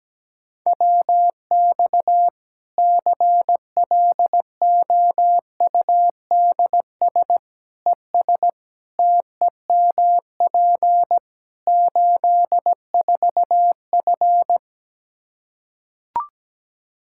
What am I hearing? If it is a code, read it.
Morse code, heard as a keyed tone, about 17 wpm, 700 Hz. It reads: WX CLOUDS ES TEMP 84F
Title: Practice QSO - Exchange 2 - Info - 17wpm